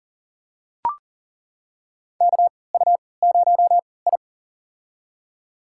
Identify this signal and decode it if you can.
Morse code, heard as a keyed tone, about 40 wpm, spaced as for 23 wpm, 700 Hz. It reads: KU0I
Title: USA Call Signs - 4 Characters - 3x Character Spacing - 40wpm